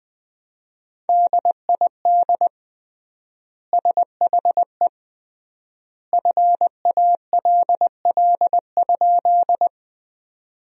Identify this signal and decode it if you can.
Morse code, heard as a keyed tone, about 20 wpm, 700 Hz. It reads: DID SHE FALL?
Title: 3x Word-Spacing - Sentences from Top 500 Words - 20wpm